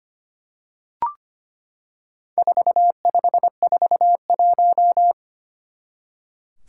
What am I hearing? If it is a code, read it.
Morse code, heard as a keyed tone, about 25 wpm, 700 Hz. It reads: 4541
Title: Four Challenging Letter-Number Combinations 25wpm